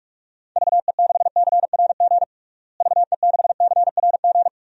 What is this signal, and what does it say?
Morse code, heard as a keyed tone, about 41 wpm, 700 Hz. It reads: VE6CRG VE6CRG
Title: DXCC-100 Call Signs - 6 Characters - Repeated 3x - 45wpm